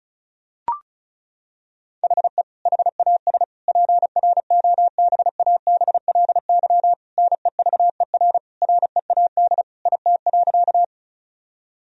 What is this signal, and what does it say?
Morse code, heard as a keyed tone, about 36 wpm, 700 Hz. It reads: HE HAS PROBABLY NEVER READ IT.
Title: Sentences from Top 1000 Words - Review Part 4 - 35wpm